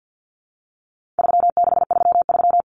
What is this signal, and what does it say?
Morse code, heard as a keyed tone, about 50 wpm, 700 Hz. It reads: 3633
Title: Common Cut Numbers - 4-Digit 50wpm